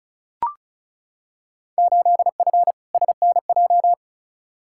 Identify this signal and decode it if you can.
Morse code, heard as a keyed tone, about 36 wpm, 690 Hz. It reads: 8F SNJ
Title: Field Day Exchanges Encore - 35wpm